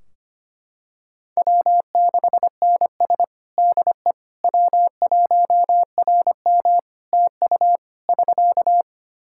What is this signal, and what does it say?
Morse code, heard as a keyed tone, about 25 wpm, 700 Hz. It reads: W6NS DE W1RM TU <SK>
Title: K1USN SST QSO Practice 25wpm